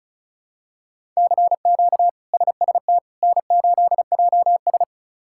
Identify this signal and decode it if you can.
Morse code, heard as a keyed tone, about 36 wpm, 700 Hz. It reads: CQ SST N8JS
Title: K1USN SST QSO Practice - 35wpm